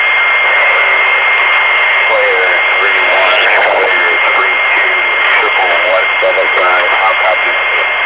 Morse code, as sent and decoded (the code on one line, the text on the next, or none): none